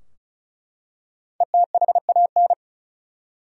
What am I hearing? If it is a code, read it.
Morse code, heard as a keyed tone, about 35 wpm, 700 Hz. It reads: ETHAN
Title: USA Common Names - Max 5 Letters - 35wpm